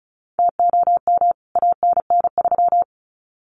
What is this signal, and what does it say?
Morse code, heard as a keyed tone, about 34 wpm, 700 Hz. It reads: TOM ANN3
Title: CWT Exchange 35wpm